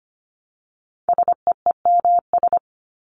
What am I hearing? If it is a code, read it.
Morse code, heard as a keyed tone, about 25 wpm, 700 Hz. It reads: SEEMS
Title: ES:S Suffix - Top 100 Words with Base Words 25wpm